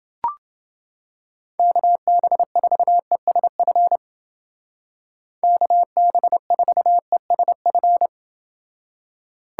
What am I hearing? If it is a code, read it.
Morse code, heard as a keyed tone, about 30 wpm, 700 Hz. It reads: KB4ESF KB4ESF